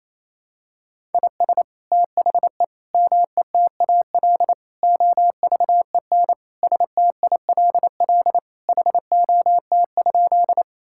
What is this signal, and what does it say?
Morse code, heard as a keyed tone, about 28 wpm, 700 Hz. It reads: IS THE METAL OVEN STILL HOT?
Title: Sentences from Top 1000 Words - Review Part 6 - 28wpm